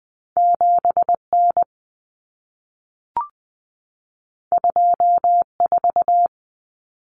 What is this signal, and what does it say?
Morse code, heard as a keyed tone, about 20 wpm, 700 Hz. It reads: 7N 24
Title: Common Cut Numbers - 2-Digit 20wpm